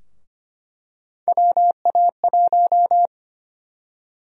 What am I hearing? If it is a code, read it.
Morse code, heard as a keyed tone, about 25 wpm, 700 Hz. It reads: WA1
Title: USA Call Sign Prefixes - 25wpm